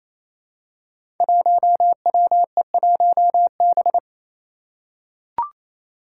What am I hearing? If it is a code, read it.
Morse code, heard as a keyed tone, about 28 wpm, 700 Hz. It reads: 1WE1B
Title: Five Challenging Letter-Number Combinations 28wpm